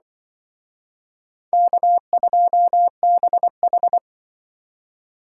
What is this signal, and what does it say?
Morse code, heard as a keyed tone, about 25 wpm, 700 Hz. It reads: K2BH